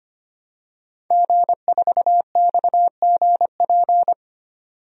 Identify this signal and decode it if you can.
Morse code, heard as a keyed tone, about 25 wpm, 700 Hz. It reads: G4XGP
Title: DXCC-100 Call Signs - 5 Characters - 25wpm